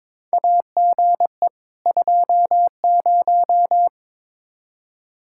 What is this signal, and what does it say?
Morse code, heard as a keyed tone, about 21 wpm, 700 Hz. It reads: AGE 20
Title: Practice QSO - Exchange 3 - More Info - 22wpm